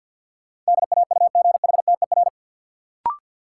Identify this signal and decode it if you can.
Morse code, heard as a keyed tone, about 52 wpm, 700 Hz. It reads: DAUGHTER
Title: Sentences from Top 1500 Words - New Word Drill Alt - 50wpm